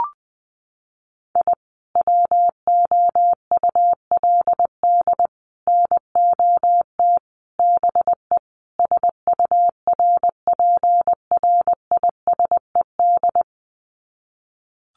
Morse code, heard as a keyed tone, about 20 wpm, 700 Hz. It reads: I WOULD NOT BE SURPRISED
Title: Sets of 5 Words Encore2 20wpm